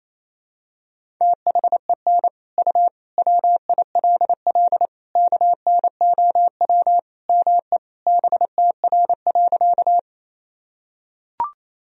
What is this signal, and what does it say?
Morse code, heard as a keyed tone, about 28 wpm, 700 Hz. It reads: THEN U WILL KNOW ME BTR.
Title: Sentences from Top 400 Words and Abbreviations 28wpm